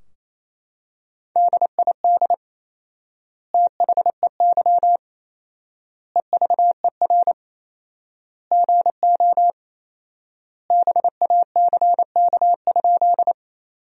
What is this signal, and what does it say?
Morse code, heard as a keyed tone, about 28 wpm, 700 Hz. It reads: DID THEY EVER GO BACK?
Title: 4x Word-Spacing - Sentences from Top 400 Words - 28wpm